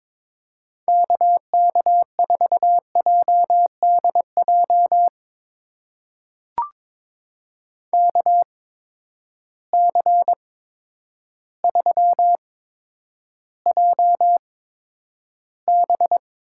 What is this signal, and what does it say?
Morse code, heard as a keyed tone, about 22 wpm, spaced as for 10 wpm, 700 Hz. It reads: KK4JDJ KC3JB
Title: USA Call Signs - 6 Characters - 8x Character Spacing - 22wpm